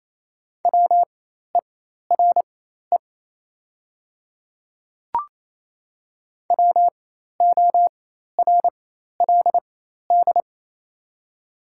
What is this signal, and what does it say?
Morse code, heard as a keyed tone, about 27 wpm, spaced as for 13 wpm, 690 Hz. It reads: WERE WORLD
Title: 4x Character Spacing - Top 200 Words - 28wpm